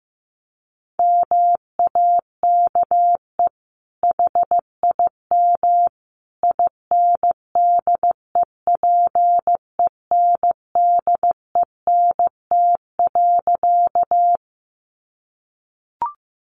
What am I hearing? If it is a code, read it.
Morse code, heard as a keyed tone, about 15 wpm, 700 Hz. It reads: MAKE HIM INDEPENDENT.
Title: Sentences from Top 900 Words 15wpm